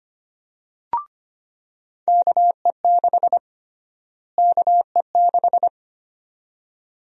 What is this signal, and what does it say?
Morse code, heard as a keyed tone, about 25 wpm, 700 Hz. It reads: KE6 KE6
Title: USA Call Sign Prefixes - Repeated 2x - 25wpm